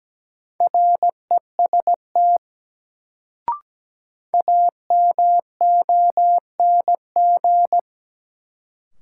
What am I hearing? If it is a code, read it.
Morse code, heard as a keyed tone, about 17 wpm, 700 Hz. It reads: REST AMONG
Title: Sentences from Top 500 Words - New Word Drill - 17wpm